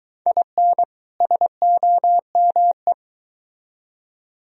Morse code, heard as a keyed tone, about 22 wpm, 700 Hz. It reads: IN SOME